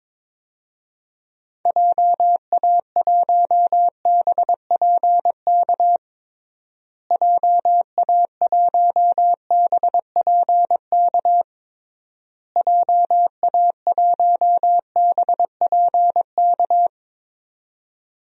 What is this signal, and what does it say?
Morse code, heard as a keyed tone, about 22 wpm, 700 Hz. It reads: JA1BPK JA1BPK JA1BPK
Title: DXCC-100 Call Signs - 6 Characters - Repeated 3x - 22wpm